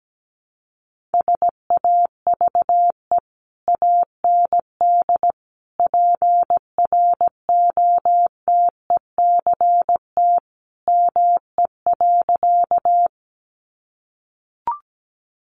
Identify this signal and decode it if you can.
Morse code, heard as a keyed tone, about 17 wpm, 700 Hz. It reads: SAVE AND PROTECT ME.
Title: Sentences from Top 1100 Words 17wpm